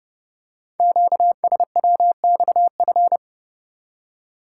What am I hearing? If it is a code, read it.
Morse code, heard as a keyed tone, about 28 wpm, 700 Hz. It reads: QSWXF
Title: Code Groups - 5 Letters - 30wpm